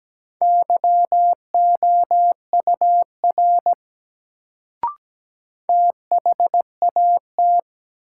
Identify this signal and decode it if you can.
Morse code, heard as a keyed tone, about 17 wpm, 700 Hz. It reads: YOUR THAT
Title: Sets of 2 Words - New Word Drill - 17wpm